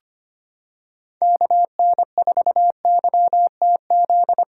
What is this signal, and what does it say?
Morse code, heard as a keyed tone, about 25 wpm, 700 Hz. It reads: KN4YTZ
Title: USA Callsign DE Callsign - 6 Characters - 25wpm